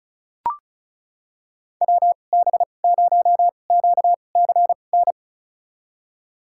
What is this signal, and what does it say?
Morse code, heard as a keyed tone, about 34 wpm, spaced as for 25 wpm, 700 Hz. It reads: WB0QCN
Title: USA Call Signs - 6 Characters - 2x Character Spacing - 35wpm